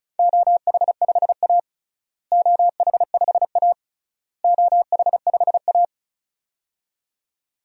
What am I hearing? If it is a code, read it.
Morse code, heard as a keyed tone, about 36 wpm, 700 Hz. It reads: OH5A OH5A OH5A
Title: DXCC-100 Call Signs - 4 Characters - Repeated 3x - 35wpm